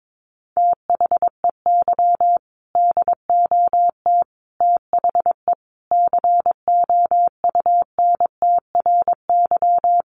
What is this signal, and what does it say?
Morse code, heard as a keyed tone, about 22 wpm, 700 Hz. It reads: THEY DOT THE COUNTRY
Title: Sentences from Top 1500 Words 22wpm